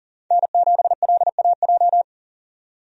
Morse code, heard as a keyed tone, about 40 wpm, 700 Hz. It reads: N7LAJ
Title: USA Call Signs - 5 Characters - 40wpm